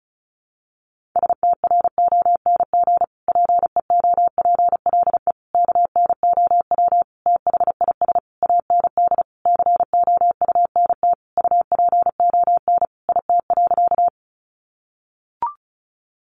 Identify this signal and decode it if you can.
Morse code, heard as a keyed tone, about 34 wpm, 700 Hz. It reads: STRONG PEOPLE KNOW THIS AND COUNT UPON IT.
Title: Sentences from Top 1500 Words 35wpm